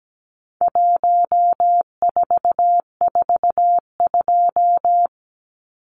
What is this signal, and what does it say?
Morse code, heard as a keyed tone, about 17 wpm, 700 Hz. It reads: 1442
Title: Numbers - 4-Digit 17wpm